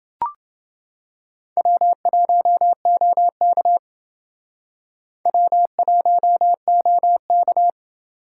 Morse code, heard as a keyed tone, about 28 wpm, 700 Hz. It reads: W1OK W1OK